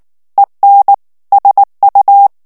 Morse code, heard as a keyed tone, about 19 wpm, 800 Hz. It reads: EN SU
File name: morse mp3 / ESPAGNOL TEXTE EN CLAIR 2 VITESSE 18 MOTS MINUTES